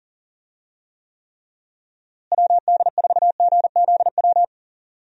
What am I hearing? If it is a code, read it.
Morse code, heard as a keyed tone, about 40 wpm, 700 Hz. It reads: WD4GZW
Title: USA Call Signs - 6 Characters - Repeated 3x - 40wpm